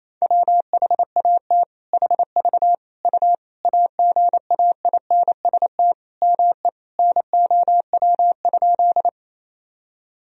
Morse code, heard as a keyed tone, about 28 wpm, 700 Hz. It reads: WHAT HV U AGAINST ME NOW?
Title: Sentences from Top 400 Words and Abbreviations 28wpm